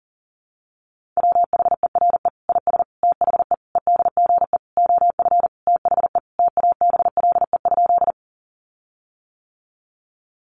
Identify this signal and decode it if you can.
Morse code, heard as a keyed tone, about 37 wpm, 700 Hz. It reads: WHERE IS THE EDGE OF THE TABLE?
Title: Sentences from Top 1300 Words 40wpm